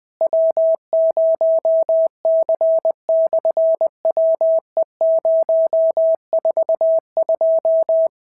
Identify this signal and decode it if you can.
Morse code, heard as a keyed tone, about 20 wpm, 650 Hz. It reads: W0C/WE042
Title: SOTA QSO Practice 20wpm